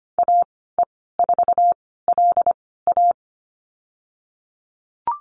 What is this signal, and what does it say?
Morse code, heard as a keyed tone, about 25 wpm, spaced as for 16 wpm, 700 Hz. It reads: AE4LA
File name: Lesson 15g - Number 5 - Call Signs 25wpm